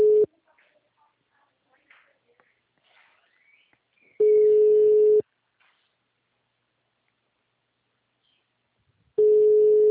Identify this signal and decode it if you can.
Ringback tone (Europe, 425 Hz).
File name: out- -2020- - - .40